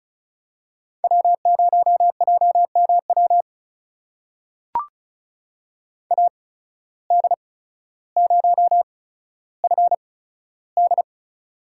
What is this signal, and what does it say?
Morse code, heard as a keyed tone, about 34 wpm, spaced as for 15 wpm, 700 Hz. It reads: W0JMW AD0FD
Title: USA Call Signs - 5 Characters - 8x Character Spacing - 35wpm